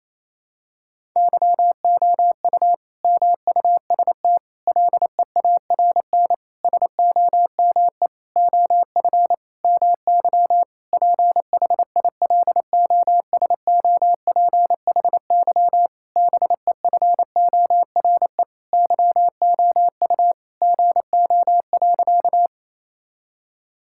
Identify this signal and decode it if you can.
Morse code, heard as a keyed tone, about 28 wpm, 700 Hz. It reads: YOU MUST LEARN SOME OF MY PHILOSOPHY BEFORE YOU GO.
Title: Sentences from Top 1000 Words - Review Part 6 - 28wpm